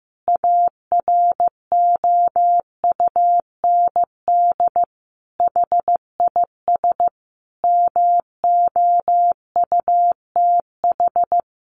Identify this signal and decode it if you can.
Morse code, heard as a keyed tone, about 15 wpm, 700 Hz. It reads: AROUND HIS MOUTH
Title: A Sound Of Thunder 15wpm